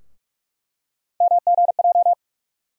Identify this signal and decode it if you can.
Morse code, heard as a keyed tone, about 47 wpm, 700 Hz. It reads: MGJ
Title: USA Call Sign Postfixes - 3 Letters 45wpm